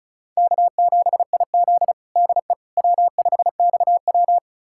Morse code, heard as a keyed tone, about 36 wpm, 700 Hz. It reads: K7IZ DE W5XW
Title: USA Callsign DE Callsign - 4 Characters - 35wpm